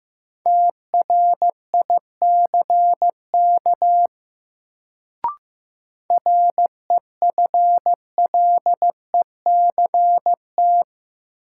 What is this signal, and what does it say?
Morse code, heard as a keyed tone, about 15 wpm, 700 Hz. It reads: TRICK REFLECT
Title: Sentences from Top 1500 Words - New Word Drill Alt - 15wpm